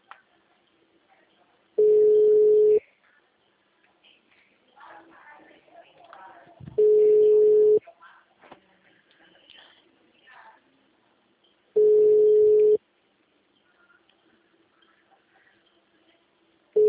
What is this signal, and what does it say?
Ringback tone (Europe, 425 Hz), with speech.